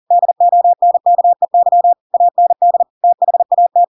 Morse code, heard as a keyed tone, about 40 wpm, 700 Hz. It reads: DONKEY AND THAT
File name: Pinocchio / Pinocchio ch4 40 40 ik1hge